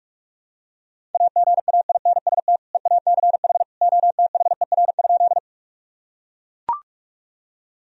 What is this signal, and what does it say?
Morse code, heard as a keyed tone, about 41 wpm, 700 Hz. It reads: AGAINST EACH OTHER?
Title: Sentences from Top 1000 Words - Review Part 1 - 45wpm